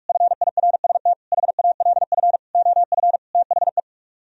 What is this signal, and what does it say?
Morse code, heard as a keyed tone, about 47 wpm, 700 Hz. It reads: FIRST HALF OF THE